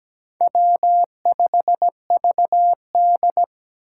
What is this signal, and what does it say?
Morse code, heard as a keyed tone, about 20 wpm, 700 Hz. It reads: W5VD
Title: Speed Racing - USA Call Signs 17wpm